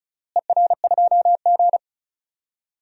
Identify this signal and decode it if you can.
Morse code, heard as a keyed tone, about 34 wpm, 700 Hz. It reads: ER2G
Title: DXCC-100 Call Signs - 4 Characters - Repeated 3x - 35wpm